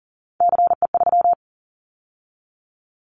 Morse code, heard as a keyed tone, about 40 wpm, 700 Hz. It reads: CE3
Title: Callsign Prefixes for Top 100 DXCC - 40wpm